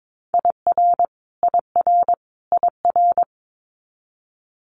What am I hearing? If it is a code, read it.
Morse code, heard as a keyed tone, about 22 wpm, 700 Hz. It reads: IR IR IR
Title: Multi-letter Phonemes Repeated 3x 22wpm